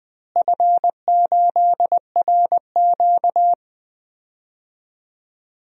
Morse code, heard as a keyed tone, about 21 wpm, 700 Hz. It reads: F8RQ
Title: DXCC-100 Call Signs - 4 Characters - Repeated 3x - 20wpm